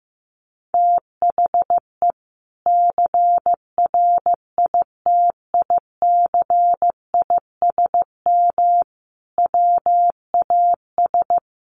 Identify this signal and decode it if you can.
Morse code, heard as a keyed tone, about 15 wpm, 700 Hz. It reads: THE CRITICISM WAS
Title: Sentences from Top 1300 Words 15wpm